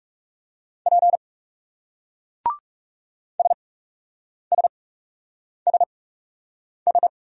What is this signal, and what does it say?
Morse code, heard as a keyed tone, about 37 wpm, spaced as for 8 wpm, 700 Hz. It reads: P SSSS
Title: Speed-Racing Single Letters 30wpm